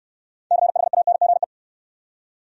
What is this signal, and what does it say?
Morse code, heard as a keyed tone, about 63 wpm, 700 Hz. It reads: CHANGE